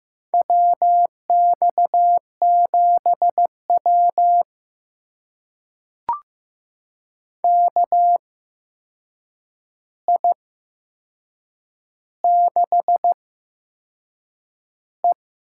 Morse code, heard as a keyed tone, about 15 wpm, spaced as for 6 wpm, 700 Hz. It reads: WX7W KI6E
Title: USA Call Signs - 4 Characters - 8x Character Spacing - 15wpm